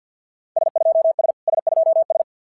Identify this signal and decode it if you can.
Morse code, heard as a keyed tone, about 52 wpm, 660 Hz. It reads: S2S S2S
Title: SOTA QSO Practice 50wpm